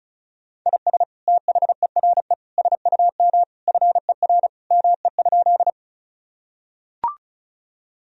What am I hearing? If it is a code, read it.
Morse code, heard as a keyed tone, about 36 wpm, 700 Hz. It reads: IS THERE SUM FER ME?
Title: Sentences from Top 200 Words and Abbreviations 35wpm